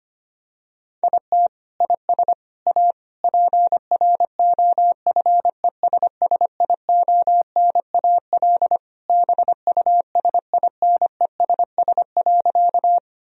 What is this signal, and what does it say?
Morse code, heard as a keyed tone, about 25 wpm, 700 Hz. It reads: IT IS A PROFESSIONAL BUSINESS.
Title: Sentences from Top 1000 Words - Review Part 4 - 25wpm